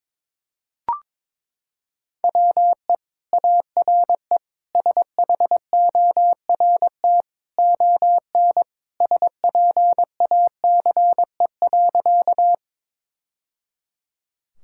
Morse code, heard as a keyed tone, about 22 wpm, 700 Hz. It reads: WE ARE SHORT ON SPACE.
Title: Sentences from Top 1000 Words - Review Part 5 - 22wpm